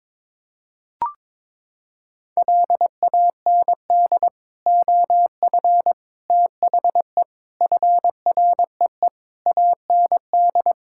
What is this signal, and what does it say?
Morse code, heard as a keyed tone, about 22 wpm, 700 Hz. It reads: LAND OF THE FREE AND
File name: Sets of 9 Words - 22wpm